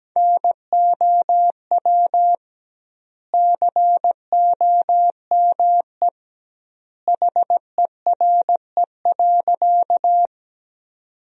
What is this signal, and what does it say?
Morse code, heard as a keyed tone, about 17 wpm, 700 Hz. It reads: NOW COME HERE.
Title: 2x Word-Spacing - Sentences from Top 200 Words - 17wpm